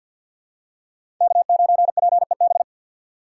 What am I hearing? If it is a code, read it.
Morse code, heard as a keyed tone, about 48 wpm, 700 Hz. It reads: K9PEB
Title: USA Call Signs - 5 Characters - 2x Character Spacing - 50wpm